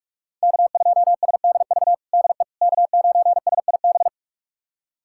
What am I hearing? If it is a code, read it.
Morse code, heard as a keyed tone, about 44 wpm, 700 Hz. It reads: K2SDV DE K9SIB
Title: USA Callsign DE Callsign - 5 Characters - 45wpm